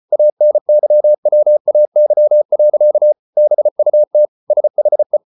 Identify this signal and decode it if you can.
Morse code, heard as a keyed tone, about 34 wpm, 600 Hz. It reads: ANYWAY. BUT SHE